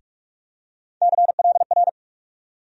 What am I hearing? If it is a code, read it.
Morse code, heard as a keyed tone, about 43 wpm, 700 Hz. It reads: CLR
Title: USA Call Sign Postfixes - 3 Letters 45wpm